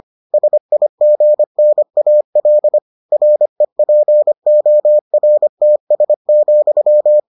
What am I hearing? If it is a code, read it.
Morse code, heard as a keyed tone, about 25 wpm, 600 Hz. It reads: SIGNAL REPORTS,